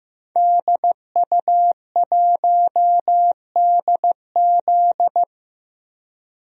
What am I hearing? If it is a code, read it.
Morse code, heard as a keyed tone, about 14 wpm, 700 Hz. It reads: DU1DZ
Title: DXCC-100 Call Signs - 5 Characters - Repeated 3x - 15wpm